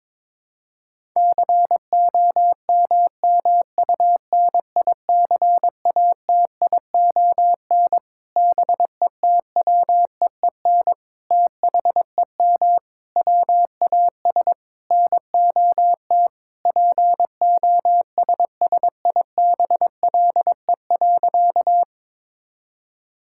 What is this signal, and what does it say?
Morse code, heard as a keyed tone, about 22 wpm, 700 Hz. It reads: COMMUNICATION BETWEEN THEM WAS NOT POSSIBLE.
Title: Sentences from Top 1000 Words - Review Part 6 - 22wpm